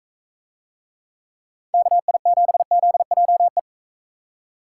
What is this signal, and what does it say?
Morse code, heard as a keyed tone, about 41 wpm, 700 Hz. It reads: KI7ZJE